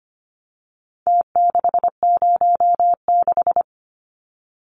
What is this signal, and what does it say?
Morse code, heard as a keyed tone, about 25 wpm, 700 Hz. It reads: T606
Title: Numbers - Contest Style - 4-Digit 25wpm